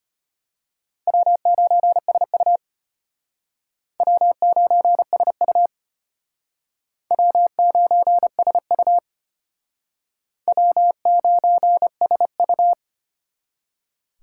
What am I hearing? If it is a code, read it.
Morse code, heard as a keyed tone, about 34 wpm, 700 Hz. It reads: W9SU W9SU W9SU W9SU